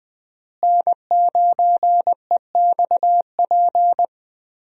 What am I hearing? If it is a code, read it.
Morse code, heard as a keyed tone, about 20 wpm, 700 Hz. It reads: N9EXP
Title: USA Call Signs - 5 Characters - Repeated 3x - 20wpm